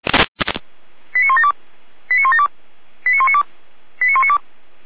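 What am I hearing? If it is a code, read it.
Five-tone selective call (ZVEI), heard as quick runs of five tones, tones of about 70 ms. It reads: 89172 89172 89192 89192
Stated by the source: Five-tone ZVEI selective call received on SDR.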